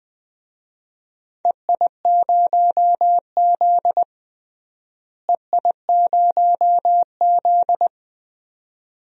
Morse code, heard as a keyed tone, about 20 wpm, 700 Hz. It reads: EI0Z EI0Z
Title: DXCC-100 Call Signs - 4 Characters - Repeated 2x - 20wpm